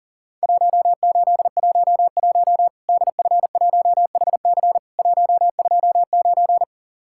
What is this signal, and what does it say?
Morse code, heard as a keyed tone, about 40 wpm, 700 Hz. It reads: 1811 DF1HC 129
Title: Worked All Europe DX Contest - QTC Exchange - 40wpm